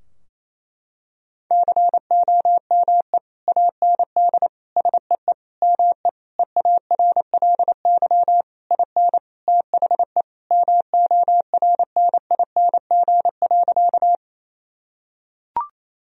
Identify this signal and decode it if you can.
Morse code, heard as a keyed tone, about 28 wpm, 700 Hz. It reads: COME AND SEE ME EARLY IN THE MORNING.
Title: Sentences from Top 1000 Words - Review Part 2 - 28wpm